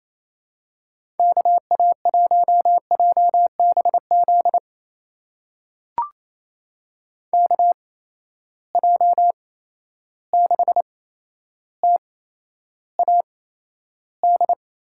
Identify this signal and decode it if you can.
Morse code, heard as a keyed tone, about 28 wpm, spaced as for 12 wpm, 700 Hz. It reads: KA1JBZ KJ6TAD
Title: USA Call Signs - 6 Characters - 8x Character Spacing - 28wpm